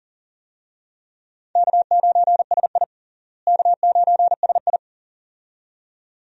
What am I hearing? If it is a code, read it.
Morse code, heard as a keyed tone, about 40 wpm, 700 Hz. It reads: K9SI K9SI
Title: USA Call Signs - 4 Characters - Repeated 2x - 40wpm